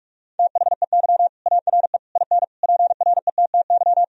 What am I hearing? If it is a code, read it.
Morse code, heard as a keyed tone, about 46 wpm, 700 Hz. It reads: THEY ARE IN PRETTY
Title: Sentences from Top 1000 Words - Review Part 3 - 45wpm